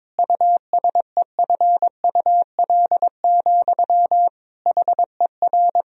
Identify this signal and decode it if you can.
Morse code, heard as a keyed tone, about 22 wpm, 700 Hz. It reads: USEFUL, HER
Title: Sentences from Top 1000 Words - Review Part 4 - 22wpm